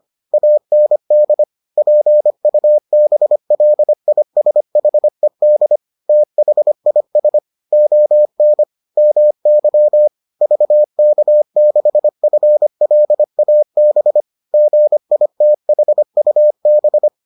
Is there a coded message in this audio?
Yes: Morse code from 0.3 to 17.1 s, heard as a keyed tone, about 25 wpm, 600 Hz. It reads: AND PUBLISHED THIS ON MY VK6FLAB GITHUB